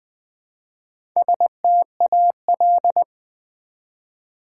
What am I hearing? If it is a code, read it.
Morse code, heard as a keyed tone, about 20 wpm, 700 Hz. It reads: STAL
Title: Build A Word - 1 Letter At A Time - 5-Letter Words - 20wpm